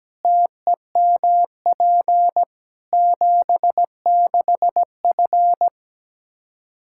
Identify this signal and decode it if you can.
Morse code, heard as a keyed tone, about 17 wpm, 700 Hz. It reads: TEMP 76F
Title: K1USN SST QSO Practice 17wpm